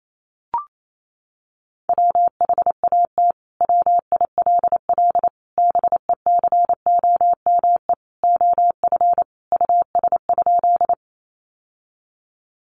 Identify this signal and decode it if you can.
Morse code, heard as a keyed tone, about 28 wpm, 700 Hz. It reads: WHAT WILL BECOME OF US?
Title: Sentences from Top 600 Words 28wpm